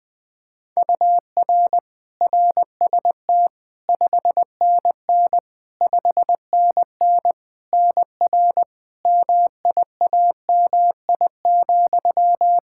Morse code, heard as a keyed tone, about 20 wpm, 700 Hz. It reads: UR RST 5NN 5NN NR MIAMI,
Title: K1USN SST QSO Practice 20wpm